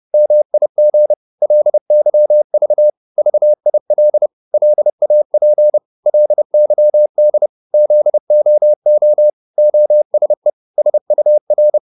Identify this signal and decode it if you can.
Morse code, heard as a keyed tone, about 30 wpm, 600 Hz. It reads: MIG LYV VIL LAP LYD ZOO OSE SUR
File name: Korte ord DK 30wpm